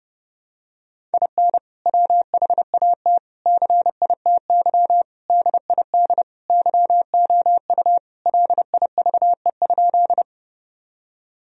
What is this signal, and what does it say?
Morse code, heard as a keyed tone, about 32 wpm, 700 Hz. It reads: IN WHAT CITY DID YOU LIVE?
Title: Sentences from Top 1000 Words - Review Part 1 - 30wpm